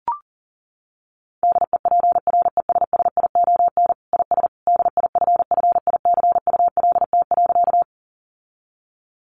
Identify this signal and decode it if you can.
Morse code, heard as a keyed tone, about 40 wpm, 700 Hz. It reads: DEPRESSION IS DIFFICULT.
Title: Sentences from Top 1300 Words 40wpm